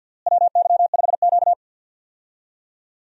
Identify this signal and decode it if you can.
Morse code, heard as a keyed tone, about 53 wpm, 700 Hz. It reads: WY5Q
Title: USA Call Signs - 4 Characters - Repeated 2x - 50wpm